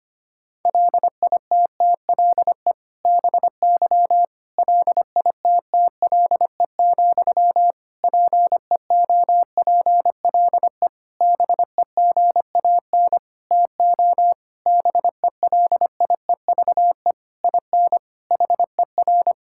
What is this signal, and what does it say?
Morse code, heard as a keyed tone, about 25 wpm, 700 Hz. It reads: LITTLE BY LITTLE, PEOPLE BEGAN TO BELIEVE IN HER
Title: Sentences from Top 1600 Words 25wpm